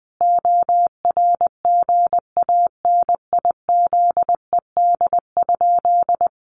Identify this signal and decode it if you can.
Morse code, heard as a keyed tone, about 20 wpm, 700 Hz. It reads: ORGANIZED?
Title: Sentences from Top 1300 Words 20wpm